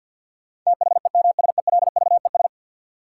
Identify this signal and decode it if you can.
Morse code, heard as a keyed tone, about 51 wpm, 700 Hz. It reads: THEMSELVES
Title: Sentences from Top 800 Words - New Word Drill - 50wpm